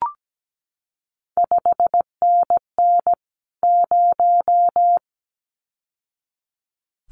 Morse code, heard as a keyed tone, about 17 wpm, 700 Hz. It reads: 5NN 0
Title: CQ WW DX Contest Exchange Encore - 17wpm